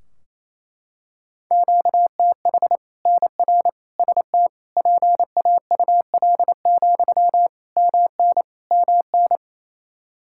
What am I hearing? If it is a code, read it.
Morse code, heard as a keyed tone, about 28 wpm, 700 Hz. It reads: QTH NR ST PAUL, MN MN
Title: Practice QSO - Exchange 1 - Beginning - 28wpm